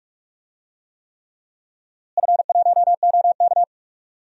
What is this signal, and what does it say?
Morse code, heard as a keyed tone, about 46 wpm, 700 Hz. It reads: F1OK
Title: DXCC-100 Call Signs - 4 Characters - Repeated 3x - 45wpm